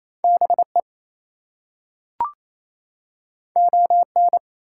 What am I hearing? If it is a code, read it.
Morse code, heard as a keyed tone, about 28 wpm, 700 Hz. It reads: BE ON
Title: Two Letter Words - 28wpm